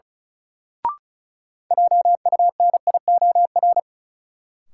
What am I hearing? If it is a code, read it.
Morse code, heard as a keyed tone, about 34 wpm, 700 Hz. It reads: JUNIOR
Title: Sentences from Top 1000 Words - Review Part 7 - New Word Drill - 35wpm